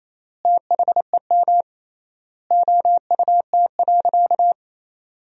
Morse code, heard as a keyed tone, about 28 wpm, 700 Hz. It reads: THEM OUT.
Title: 3x Word-Spacing - Sentences from Top 500 Words - 28wpm